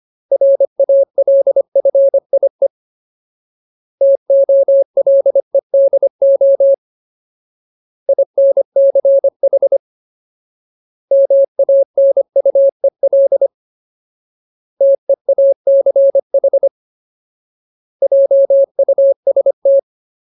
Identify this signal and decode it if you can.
Morse code, heard as a keyed tone, about 25 wpm, 550 Hz. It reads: RALFIE TOLEDO INCH MANUEL TEACH JUST
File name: Ultimate Word List / 25WPM Ultimate